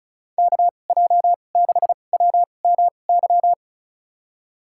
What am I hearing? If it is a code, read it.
Morse code, heard as a keyed tone, about 34 wpm, spaced as for 25 wpm, 700 Hz. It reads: KJ6WMY
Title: USA Call Signs - 6 Characters - 2x Character Spacing - 35wpm